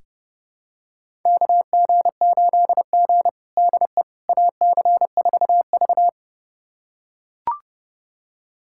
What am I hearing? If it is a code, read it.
Morse code, heard as a keyed tone, about 32 wpm, 700 Hz. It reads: KG8G DE AC4V
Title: USA Callsign DE Callsign - 4 Characters - 30wpm